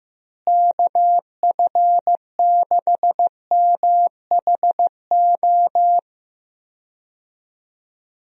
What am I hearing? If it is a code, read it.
Morse code, heard as a keyed tone, about 15 wpm, 700 Hz. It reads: KF6MHO
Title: USA Call Signs - 6 Characters - Repeated 2x - 15wpm